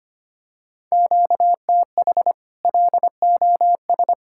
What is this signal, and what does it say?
Morse code, heard as a keyed tone, about 25 wpm, 700 Hz. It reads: QTH LOS
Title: Practice QSO - Exchange 1 - Beginning - 25wpm